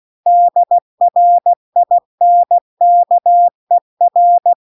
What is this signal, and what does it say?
Morse code, heard as a keyed tone, about 16 wpm, 700 Hz. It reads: DRINKER